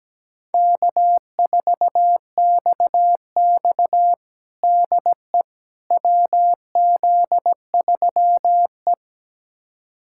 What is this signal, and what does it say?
Morse code, heard as a keyed tone, about 17 wpm, 700 Hz. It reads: K4XX DE WZ3E
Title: USA Callsign DE Callsign - 4 Characters - 17wpm